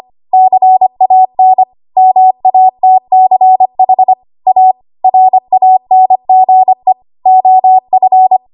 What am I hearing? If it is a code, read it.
Morse code, heard as a keyed tone, about 25 wpm, 750 Hz. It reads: CAN MATCH A RANGE OF
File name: Archive / 25 WPM / 25WPM